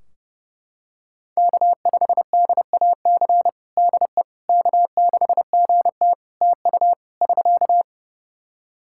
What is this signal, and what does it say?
Morse code, heard as a keyed tone, about 30 wpm, 700 Hz. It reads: K5DAC DE K6GT TU <SK>
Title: Practice QSO - Exchange 4 - Ending - 30wpm